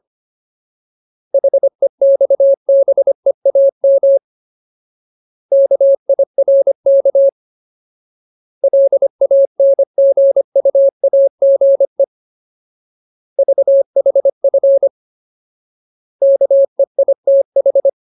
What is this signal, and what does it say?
Morse code, heard as a keyed tone, about 25 wpm, 550 Hz. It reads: HEXBEAM KIRK LANGUAGE VHF KEITH
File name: Ultimate Word List / 25WPM Ultimate